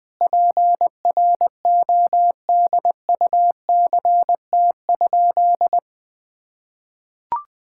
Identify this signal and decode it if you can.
Morse code, heard as a keyed tone, about 20 wpm, 700 Hz. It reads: PRODUCT?
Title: Sentences with 9-Word Phrases - Spoken Beginning - 20wpm